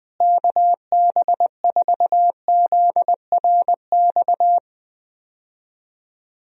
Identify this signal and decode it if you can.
Morse code, heard as a keyed tone, about 20 wpm, 700 Hz. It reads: KB4ZRX
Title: USA Call Signs - 6 Characters - Repeated 2x - 20wpm